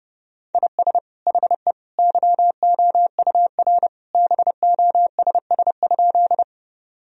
Morse code, heard as a keyed tone, about 30 wpm, 700 Hz. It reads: IS HE YOUR BOSS?
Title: Sentences from Top 1000 Words - Review Part 4 - 30wpm